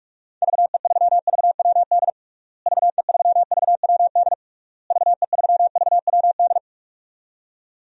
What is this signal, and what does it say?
Morse code, heard as a keyed tone, about 46 wpm, 700 Hz. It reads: VE3VWD VE3VWD VE3VWD
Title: DXCC-100 Call Signs - 6 Characters - Repeated 3x - 45wpm